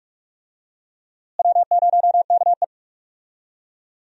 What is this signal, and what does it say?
Morse code, heard as a keyed tone, about 46 wpm, 700 Hz. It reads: W0KE
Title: USA Call Signs - 4 Characters - Repeated 2x - 45wpm